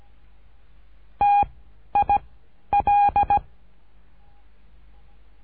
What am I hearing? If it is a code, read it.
Morse code, heard as a keyed tone, about 17 wpm, 820 Hz. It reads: T I L
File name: cursos / CURSO DE TELEGRAFIA / Telegrafia - Trilha 13